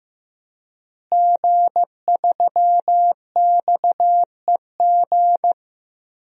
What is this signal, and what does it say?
Morse code, heard as a keyed tone, about 15 wpm, 700 Hz. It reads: G3XEG
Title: DXCC-100 Call Signs - 5 Characters - Repeated 3x - 15wpm